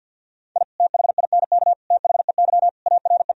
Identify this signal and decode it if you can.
Morse code, heard as a keyed tone, about 50 wpm, 700 Hz. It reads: I THINK THEY ARE
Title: Sentences from Top 1000 Words - Review Part 4 - 50wpm